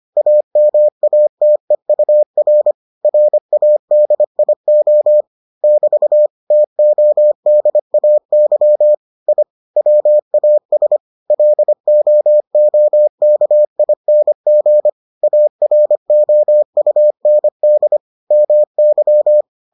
Morse code, heard as a keyed tone, about 25 wpm, 600 Hz. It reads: AMATEUR RADIO <BT> TODAY I WAS LOOKING AROUND MY